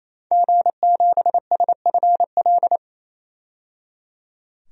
Morse code, heard as a keyed tone, about 28 wpm, 700 Hz. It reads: G7SFL
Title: DXCC-100 Call Signs - 5 Characters - 28wpm